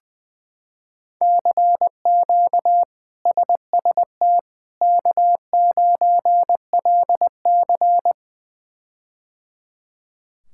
Morse code, heard as a keyed tone, about 20 wpm, 700 Hz. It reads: CQ SST K9LC
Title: K1USN SST QSO Practice - 20wpm